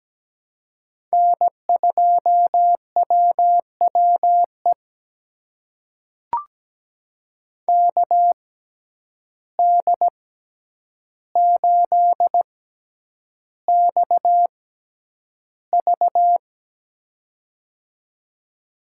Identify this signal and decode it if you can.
Morse code, heard as a keyed tone, about 17 wpm, spaced as for 9 wpm, 700 Hz. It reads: N2WWE KD8XV
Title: USA Call Signs - 5 Characters - 6x Character Spacing - 17wpm